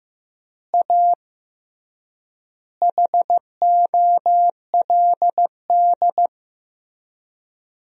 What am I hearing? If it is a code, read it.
Morse code, heard as a keyed tone, about 15 wpm, 700 Hz. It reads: A HOLD
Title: 3x Word-Spacing - Sentences from Top 400 Words - 15wpm